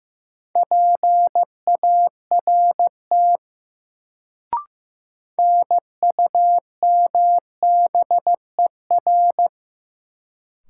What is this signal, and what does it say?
Morse code, heard as a keyed tone, about 15 wpm, 700 Hz. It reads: PART NUMBER
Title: Sets of 2 Words - New Word Drill - 15wpm